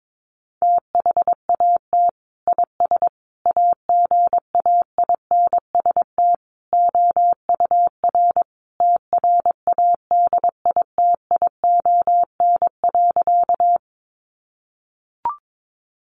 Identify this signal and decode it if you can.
Morse code, heard as a keyed tone, about 22 wpm, 700 Hz. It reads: THAT IS AGAINST OUR TRADITION.
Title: Sentences from Top 1500 Words 22wpm